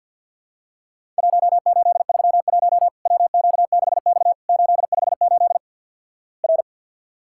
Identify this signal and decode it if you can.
Morse code, heard as a keyed tone, about 50 wpm, 700 Hz. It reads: 1831 WQ6X 858 R
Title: Worked All Europe DX Contest - QTC Exchange - 50wpm